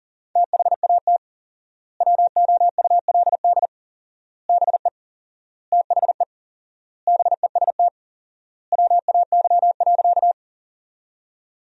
Morse code, heard as a keyed tone, about 40 wpm, 700 Hz. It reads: THAT WOULD BE THE BEST WAY.
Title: 4x Word-Spacing - Sentences from Top 400 Words - 40wpm